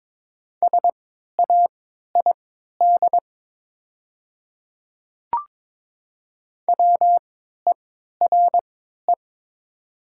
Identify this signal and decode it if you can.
Morse code, heard as a keyed tone, about 22 wpm, spaced as for 12 wpm, 700 Hz. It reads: SAID WERE
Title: 3x Character Spacing - Top 100 Words - 22wpm